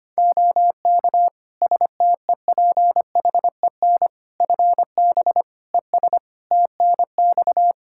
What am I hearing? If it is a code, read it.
Morse code, heard as a keyed tone, about 25 wpm, 700 Hz. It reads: OK STEPHEN FB ES TNX
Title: K1USN SST QSO Practice 25wpm